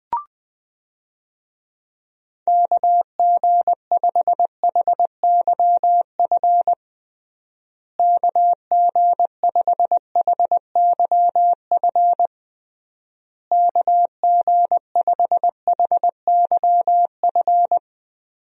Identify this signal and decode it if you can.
Morse code, heard as a keyed tone, about 20 wpm, 700 Hz. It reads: KG5HYF KG5HYF KG5HYF
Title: USA Call Signs - 6 Characters - Repeated 3x - 20wpm